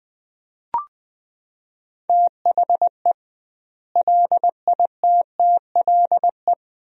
Morse code, heard as a keyed tone, about 21 wpm, 700 Hz. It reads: THE LITTLE
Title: 2x Word-Spacing - Sentences from Top 500 Words - 20wpm